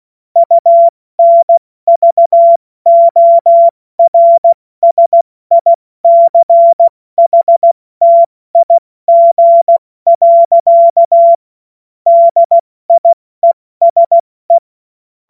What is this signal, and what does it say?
Morse code, heard as a keyed tone, about 16 wpm, 680 Hz. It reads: UNVORSICHTIG. DIESE
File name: Morsekurs / Witze / Witz 02. August 2022 70 F80